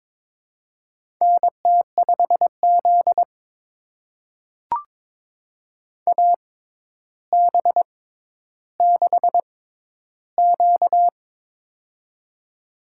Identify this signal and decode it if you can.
Morse code, heard as a keyed tone, about 22 wpm, spaced as for 11 wpm, 700 Hz. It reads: NT5Z AB6Q
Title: USA Call Signs - 4 Characters - 6x Character Spacing - 22wpm